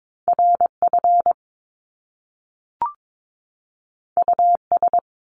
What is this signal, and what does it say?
Morse code, heard as a keyed tone, about 22 wpm, 700 Hz. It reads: RF US
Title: Top Two Letter Combinations in Words 22wpm